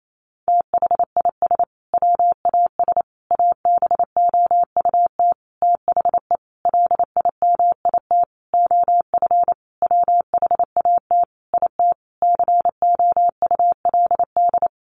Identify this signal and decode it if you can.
Morse code, heard as a keyed tone, about 28 wpm, 700 Hz. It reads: THIS WAS ABOUT THE LIMIT OF WHAT IT COULD
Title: Sentences from Top 1500 Words 28wpm